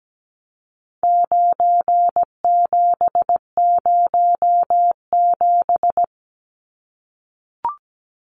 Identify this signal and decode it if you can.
Morse code, heard as a keyed tone, about 17 wpm, 700 Hz. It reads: 9707
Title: Numbers - 4-Digit 17wpm